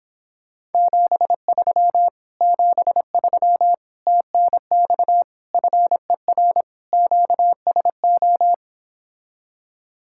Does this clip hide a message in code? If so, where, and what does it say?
Yes: Morse code from 0.7 to 8.5 s, heard as a keyed tone, about 29 wpm, 700 Hz. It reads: 73 73 TNX FER QSO